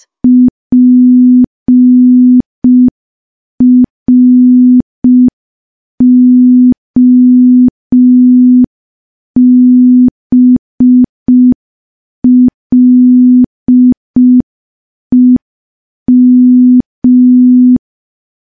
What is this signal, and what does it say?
Morse code, heard as a keyed tone, about 5 wpm, 250 Hz. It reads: PROBLEM
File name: morse / morse-training-05WPM